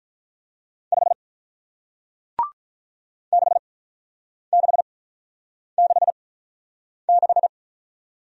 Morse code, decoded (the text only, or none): 5 6666